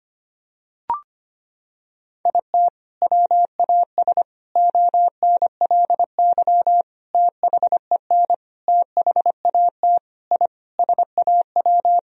Morse code, heard as a keyed tone, about 25 wpm, 700 Hz. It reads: IT WAS ONLY THEN THAT I SAW
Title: Sentences from Top 1000 Words - Review Part 3 - 25wpm